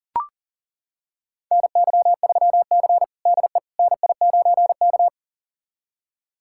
Morse code, heard as a keyed tone, about 40 wpm, 700 Hz. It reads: NY3C DE NI9K
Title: USA Callsign DE Callsign - 4 Characters - 40wpm